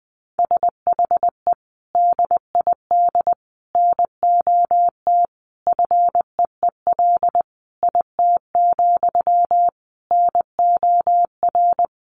Morse code, heard as a keyed tone, about 20 wpm, 700 Hz. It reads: SHE DID NOT FEEL IT, NOR
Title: Sentences from Top 1500 Words 20wpm